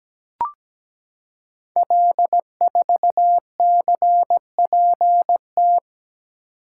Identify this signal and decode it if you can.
Morse code, heard as a keyed tone, about 17 wpm, 700 Hz. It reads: L4CPT
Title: Code Groups - 5 Letter:Numbers - 17wpm